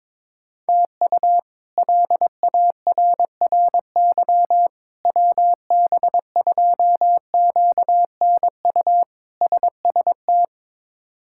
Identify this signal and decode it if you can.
Morse code, heard as a keyed tone, about 22 wpm, 700 Hz. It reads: TU LARRY WB2QNU SST
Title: K1USN SST QSO Practice - 22wpm